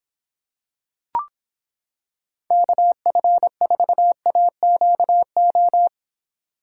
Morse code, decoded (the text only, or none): KF4AQO